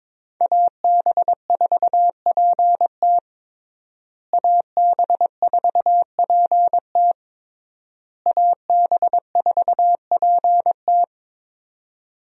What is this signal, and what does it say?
Morse code, heard as a keyed tone, about 22 wpm, 700 Hz. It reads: AB4PT AB4PT AB4PT
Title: USA Call Signs - 5 Characters - Repeated 3x - 22wpm